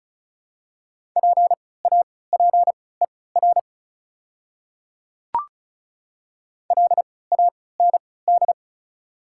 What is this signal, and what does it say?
Morse code, heard as a keyed tone, about 36 wpm, spaced as for 20 wpm, 690 Hz. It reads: PAPER LAND
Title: 3x Character Spacing - Top 300 Words - 35wpm